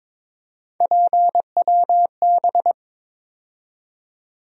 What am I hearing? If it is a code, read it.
Morse code, heard as a keyed tone, about 22 wpm, 700 Hz. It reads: PWB
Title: USA Call Sign Postfixes - 3 Letters 22wpm